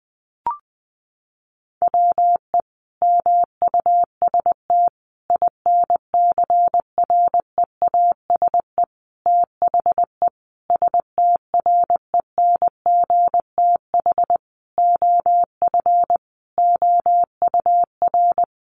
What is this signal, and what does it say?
Morse code, heard as a keyed tone, about 19 wpm, 700 Hz. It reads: WE MUST INCREASE THE STRENGTH OF OUR
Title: Sentences from Top 1300 Words 20wpm